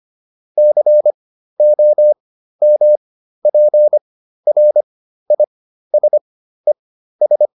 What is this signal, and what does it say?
Morse code, heard as a keyed tone, about 25 wpm, spaced as for 13 wpm, 600 Hz. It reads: COMPRISES